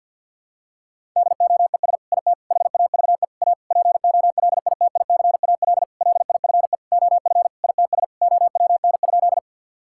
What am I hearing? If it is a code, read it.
Morse code, heard as a keyed tone, about 52 wpm, 700 Hz. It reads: DOES IT HAVE A POLITICAL LIFE OF ITS OWN?
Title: Sentences from Top 1000 Words - Review Part 5 - 50wpm